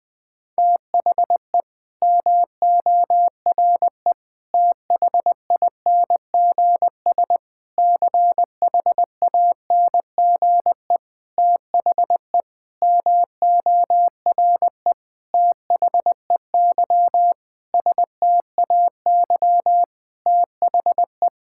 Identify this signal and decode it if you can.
Morse code, heard as a keyed tone, about 20 wpm, 700 Hz. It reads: THE MORE THINGS CHANGE THE MORE THEY STAY THE
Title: Sentences with 9-Word Phrases - 20wpm